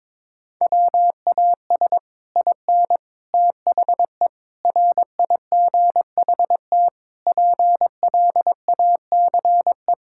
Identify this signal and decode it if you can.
Morse code, heard as a keyed tone, about 22 wpm, 700 Hz. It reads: WAS IN THE RIGHT PLACE
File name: Sets of 9 Words - 22wpm